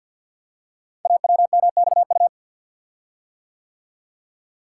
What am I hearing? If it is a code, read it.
Morse code, heard as a keyed tone, about 48 wpm, 690 Hz. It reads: AWMXU
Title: Code Groups - 5 Letters - 50wpm